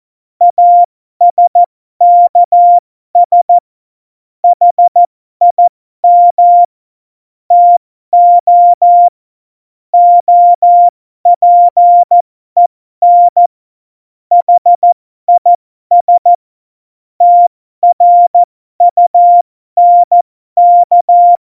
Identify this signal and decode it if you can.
Morse code, heard as a keyed tone, about 14 wpm, spaced as for 12 wpm, 700 Hz. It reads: ASKS HIM TO OPEN HIS TRUNK
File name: Morsekurs / Witze